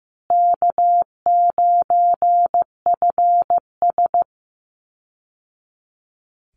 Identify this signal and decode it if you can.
Morse code, heard as a keyed tone, about 15 wpm, 700 Hz. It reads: K9FS
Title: Easy Call Signs Repeated 2x - 15wpm